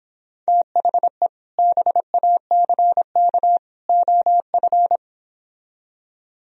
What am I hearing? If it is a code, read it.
Morse code, heard as a keyed tone, about 26 wpm, 700 Hz. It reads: THE BACK OF